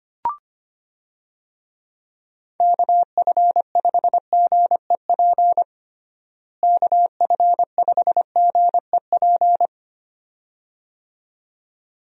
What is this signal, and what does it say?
Morse code, heard as a keyed tone, about 25 wpm, 700 Hz. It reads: KF5GEP KF5GEP
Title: USA Call Signs - 6 Characters - Repeated 2x - 25wpm